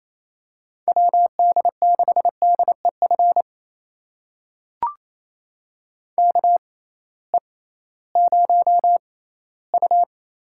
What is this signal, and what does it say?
Morse code, heard as a keyed tone, about 28 wpm, spaced as for 16 wpm, 700 Hz. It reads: WD6DEF KE0U
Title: USA Call Signs - 6 Characters - 6x Character Spacing - 28wpm